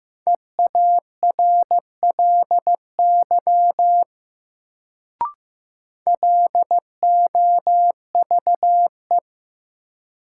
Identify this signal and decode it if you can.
Morse code, heard as a keyed tone, about 15 wpm, 700 Hz. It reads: EARLY LOVE
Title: Sentences from Top 400 Words - New Word Drill - 15wpm